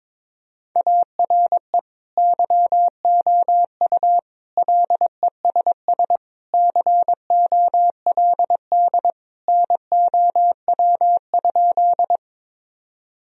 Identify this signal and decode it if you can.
Morse code, heard as a keyed tone, about 22 wpm, 700 Hz. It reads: ARE YOU LESS COLD NOW?
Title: Sentences from Top 1000 Words - Review Part 2 - 22wpm